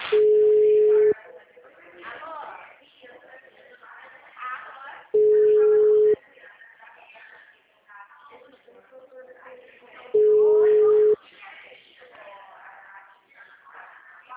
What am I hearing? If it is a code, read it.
Ringback tone (Europe, 425 Hz), with speech.
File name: out- -2045- - - .12897